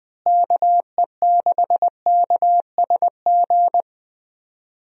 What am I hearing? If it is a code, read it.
Morse code, heard as a keyed tone, about 20 wpm, 700 Hz. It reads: KE6KSG
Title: USA Call Signs - 6 Characters - Repeated 2x - 20wpm